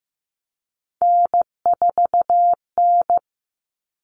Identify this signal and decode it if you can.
Morse code, heard as a keyed tone, about 15 wpm, 700 Hz. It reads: N4N
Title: Common Cut Numbers - 3-Digit 15wpm